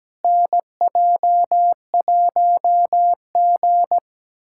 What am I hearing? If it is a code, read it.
Morse code, heard as a keyed tone, about 17 wpm, 700 Hz. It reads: NJ1G
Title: USA Callsign DE Callsign - 4 Characters - 17wpm